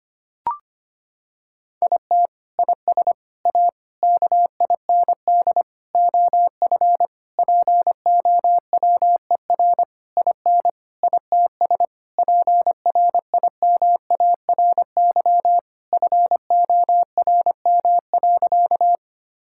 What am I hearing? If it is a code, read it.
Morse code, heard as a keyed tone, about 25 wpm, 700 Hz. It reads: IT IS A KIND OF POWER IN ITS PRIMARY FORM.
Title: Sentences from Top 1000 Words - Review Part 5 - 25wpm